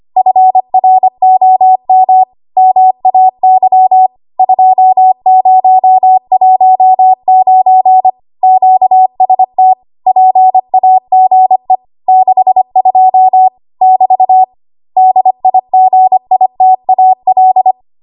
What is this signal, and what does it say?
Morse code, heard as a keyed tone, about 25 wpm, 750 Hz. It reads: FROM MAY 2019 QST PAGE 62 <BT> DIGITAL
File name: Archive / 25 WPM / 25WPM